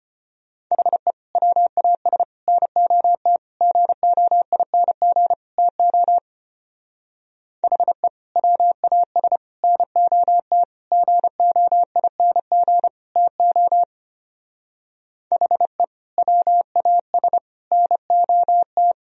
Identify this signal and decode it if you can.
Morse code, heard as a keyed tone, about 29 wpm, 700 Hz. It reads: HE WAS NOT GOING TO HE WAS NOT GOING TO HE WAS NOT
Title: Speed-Racing - Sets of 5 Words 25wpm